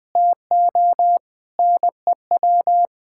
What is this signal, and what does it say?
Morse code, heard as a keyed tone, about 20 wpm, 700 Hz. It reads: TO NEW
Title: Difficult Sentences with Hints - 20wpm